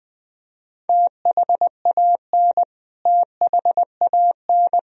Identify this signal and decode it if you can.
Morse code, heard as a keyed tone, about 20 wpm, 700 Hz. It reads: THAN THAN
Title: Top 100 Words Repeated 20wpm